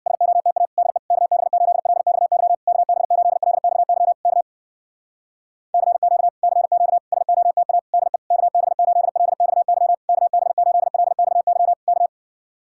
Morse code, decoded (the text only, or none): SOTA DE KB8LXY KB8LXY K CQ CQ SOTA DE KB8LXY KB8LXY K